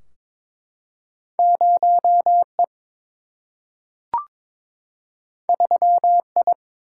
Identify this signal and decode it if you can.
Morse code, heard as a keyed tone, about 22 wpm, 700 Hz. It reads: 0 E 3 I
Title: Single-Letter Single-Number 22wpm